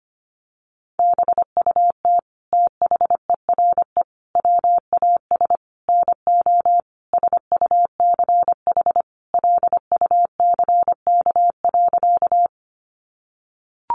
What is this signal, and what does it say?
Morse code, heard as a keyed tone, about 25 wpm, 700 Hz. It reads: BUT THERE WAS NO SUCH LUCK.
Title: Sentences from Top 1500 Words 25wpm